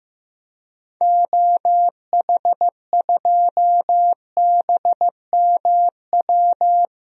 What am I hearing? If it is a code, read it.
Morse code, heard as a keyed tone, about 15 wpm, 700 Hz. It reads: OH2BMW
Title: DXCC-100 Call Signs - 6 Characters - Repeated 2x - 15wpm